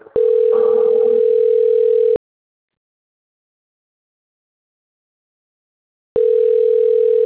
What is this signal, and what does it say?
Ringback tone (North America), 440 + 480 Hz.